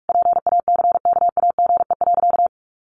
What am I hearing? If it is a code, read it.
Morse code, heard as a keyed tone, about 44 wpm, 700 Hz. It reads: PACKAGE.
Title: Easy Phrases 45wpm